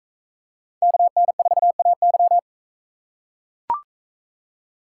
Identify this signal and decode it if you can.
Morse code, heard as a keyed tone, about 44 wpm, 700 Hz. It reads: KN4AY